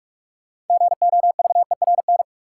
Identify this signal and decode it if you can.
Morse code, heard as a keyed tone, about 45 wpm, 700 Hz. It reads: GOVERN
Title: Sentences from Top 1000 Words - Review Part 2 - New Word Drill - 45wpm